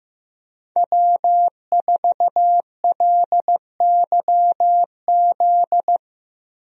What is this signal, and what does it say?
Morse code, heard as a keyed tone, about 15 wpm, 700 Hz. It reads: W4LYZ
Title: USA Call Signs - 5 Characters - Repeated 3x - 15wpm